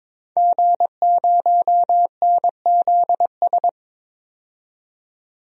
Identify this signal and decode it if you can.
Morse code, heard as a keyed tone, about 22 wpm, 700 Hz. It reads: G0NZS
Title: DXCC-100 Call Signs - 5 Characters - 22wpm